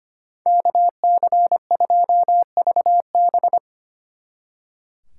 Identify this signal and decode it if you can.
Morse code, heard as a keyed tone, about 28 wpm, 700 Hz. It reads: KC2VB
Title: Speed Racing - USA 2x2 Call Signs 25wpm